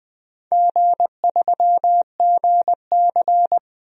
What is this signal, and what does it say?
Morse code, heard as a keyed tone, about 21 wpm, 690 Hz. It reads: G3GC
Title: DXCC-100 Call Signs - 4 Characters - Repeated 3x - 20wpm